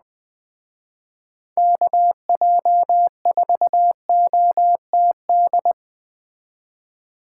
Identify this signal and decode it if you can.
Morse code, heard as a keyed tone, about 20 wpm, 700 Hz. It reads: KJ4OTD
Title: Speed Racing - USA Call Signs 17wpm